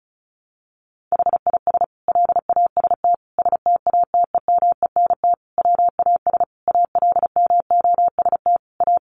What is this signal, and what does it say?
Morse code, heard as a keyed tone, about 34 wpm, 700 Hz. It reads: HIS LAST STATEMENT WAS ALMOST A
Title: Sentences from Top 1200 Words 35wpm